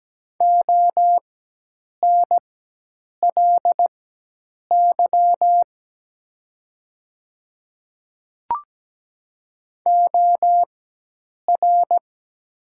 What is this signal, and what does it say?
Morse code, heard as a keyed tone, about 17 wpm, spaced as for 8 wpm, 700 Hz. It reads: ONLY OR
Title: 4x Character Spacing - Top 300 Words - 17wpm